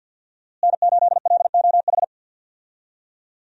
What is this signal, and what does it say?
Morse code, heard as a keyed tone, about 51 wpm, 700 Hz. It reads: N8LOH
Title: USA Call Signs - 5 Characters - Repeated 2x - 50wpm